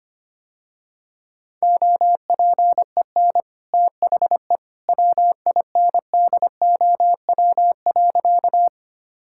Morse code, heard as a keyed tone, about 25 wpm, 700 Hz. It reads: OPEN THE WINDOW.
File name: Easy Sentences with Hints - 25wpm